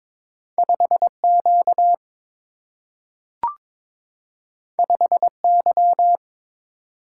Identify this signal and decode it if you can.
Morse code, heard as a keyed tone, about 23 wpm, 700 Hz. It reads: 5Q 5Y
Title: Single-Letter Single-Number 22wpm